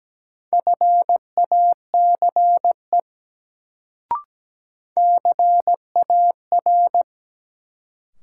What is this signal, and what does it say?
Morse code, heard as a keyed tone, about 17 wpm, 700 Hz. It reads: FACE CAR
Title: Sentences from Top 300 Words - New Word Drill - 17wpm